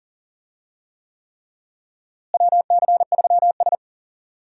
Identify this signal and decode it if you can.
Morse code, heard as a keyed tone, about 40 wpm, 700 Hz. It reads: WC3S
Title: USA Call Signs - 4 Characters - Repeated 3x - 40wpm